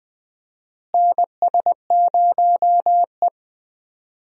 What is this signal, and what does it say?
Morse code, heard as a keyed tone, about 20 wpm, 700 Hz. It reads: NS0E
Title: USA Call Signs - 4 Characters - Repeated 2x - 20wpm